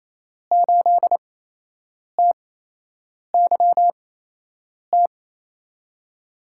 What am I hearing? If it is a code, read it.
Morse code, heard as a keyed tone, about 28 wpm, spaced as for 8 wpm, 700 Hz. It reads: 8TYT
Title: USA Call Signs - 8x Character Spacing - 28wpm